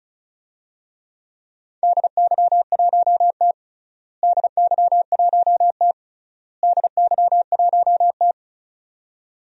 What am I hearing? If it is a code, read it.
Morse code, heard as a keyed tone, about 36 wpm, 700 Hz. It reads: DY1T DY1T DY1T
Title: DXCC-100 Call Signs - 4 Characters - Repeated 3x - 35wpm